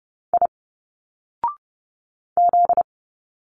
Morse code, heard as a keyed tone, about 30 wpm, 700 Hz. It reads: I Z
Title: Single Letters 30wpm